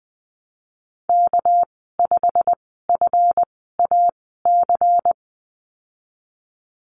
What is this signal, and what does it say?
Morse code, heard as a keyed tone, about 20 wpm, spaced as for 15 wpm, 700 Hz. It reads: K5FAC
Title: 219 Lesson 35g - Number 0 - Call Signs 20wpm